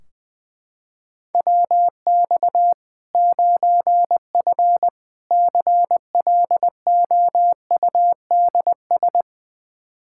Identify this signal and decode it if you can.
Morse code, heard as a keyed tone, about 20 wpm, 700 Hz. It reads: WX 9F CLOUDS
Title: K1USN SST QSO Practice 20wpm